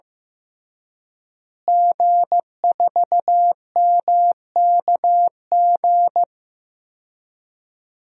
Morse code, heard as a keyed tone, about 15 wpm, 700 Hz. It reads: G4MKG
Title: DXCC-100 Call Signs - 5 Characters - Repeated 3x - 15wpm